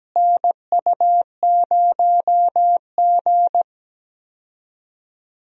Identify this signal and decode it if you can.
Morse code, heard as a keyed tone, about 17 wpm, 700 Hz. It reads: NU0G
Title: USA Call Signs - 4 Characters - Repeated 2x - 17wpm